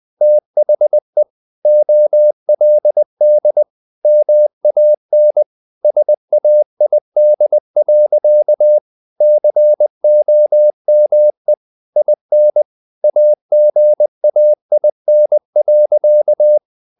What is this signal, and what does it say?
Morse code, heard as a keyed tone, about 20 wpm, 600 Hz. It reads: THE OLD MAN SAID. COME IN AGAIN.
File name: the old man and the sea part 2 20wpm